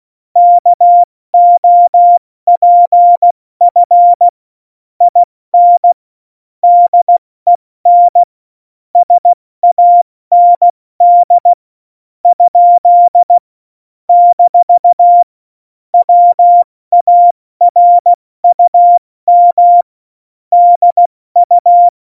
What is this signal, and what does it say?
Morse code, heard as a keyed tone, about 16 wpm, spaced as for 13 wpm, 700 Hz. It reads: KOPF IN DEN SAND? - WARUM DU
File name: Morsekurs / Witze / Witz 01. Mai 2018, 70 F80